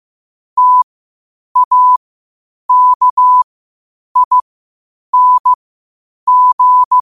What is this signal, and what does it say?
Morse code, heard as a keyed tone, about 15 wpm, spaced as for 8 wpm, 1000 Hz. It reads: TAKING
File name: qotdfiles / 8wpm